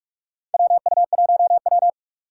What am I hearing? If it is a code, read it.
Morse code, heard as a keyed tone, about 46 wpm, 700 Hz. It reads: WU1W
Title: USA Call Signs - 4 Characters - 3x Character Spacing - 45wpm